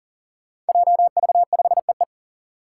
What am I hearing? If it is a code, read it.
Morse code, heard as a keyed tone, about 40 wpm, 700 Hz. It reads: JV5EE
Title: Five Challenging Letter-Number Combinations 40wpm